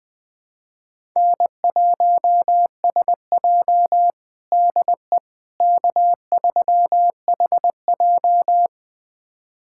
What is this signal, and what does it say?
Morse code, heard as a keyed tone, about 21 wpm, 700 Hz. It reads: N1SJ DE K3HJ
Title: USA Callsign DE Callsign - 4 Characters - 20wpm